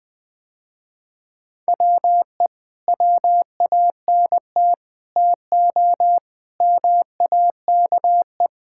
Morse code, heard as a keyed tone, about 20 wpm, 700 Hz. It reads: WE WANT TO MAKE
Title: Sentences with 5-Word Phrases - Spoken Beginning - 20wpm